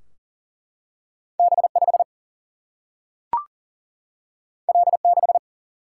Morse code, heard as a keyed tone, about 39 wpm, 700 Hz. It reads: B5 L6
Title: Single-Letter Single-Number 40wpm